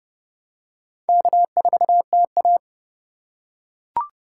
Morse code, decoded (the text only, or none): K4TA